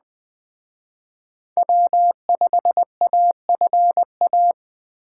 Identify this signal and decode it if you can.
Morse code, heard as a keyed tone, about 20 wpm, 700 Hz. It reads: W5AFA